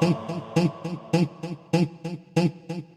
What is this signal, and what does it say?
There is no code.